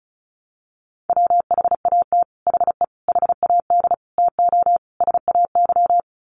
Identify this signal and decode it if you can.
Morse code, heard as a keyed tone, about 34 wpm, 700 Hz. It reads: WHAT HE HAD TO SAY
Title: Sets of 5 Words Encore2 35wpm